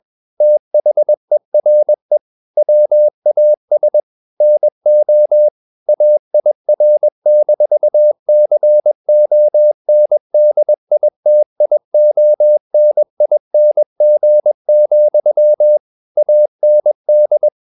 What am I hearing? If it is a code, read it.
Morse code, heard as a keyed tone, about 21 wpm, 600 Hz. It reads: THERE WAS NO AIR-CONDITIONING, AND
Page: CW med Gnister